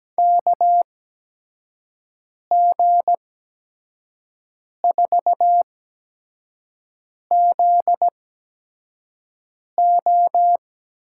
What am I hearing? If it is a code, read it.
Morse code, heard as a keyed tone, about 17 wpm, spaced as for 5 wpm, 700 Hz. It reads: KG4ZO
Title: USA Call Signs - 6 Characters - 8x Character Spacing - 17wpm